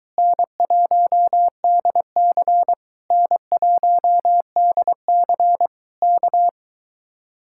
Morse code, heard as a keyed tone, about 23 wpm, 700 Hz. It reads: N1DC N1DC K